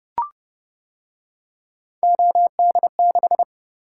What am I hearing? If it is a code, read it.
Morse code, heard as a keyed tone, about 28 wpm, 700 Hz. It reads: OD6